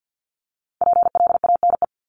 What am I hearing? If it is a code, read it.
Morse code, heard as a keyed tone, about 50 wpm, 700 Hz. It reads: PLANE
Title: Top 500 Words 50wpm